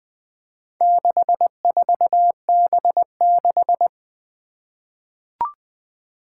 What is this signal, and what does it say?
Morse code, heard as a keyed tone, about 21 wpm, 700 Hz. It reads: 64B6
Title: Four Challenging Letter-Number Combinations 20wpm